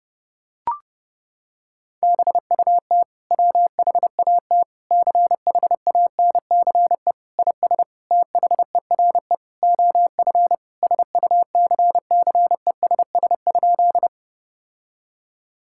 Morse code, heard as a keyed tone, about 32 wpm, 700 Hz. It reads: BUT WHAT CHANCE IS THERE OF SUCCESS?
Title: Sentences from Top 1000 Words - Review Part 6 - 30wpm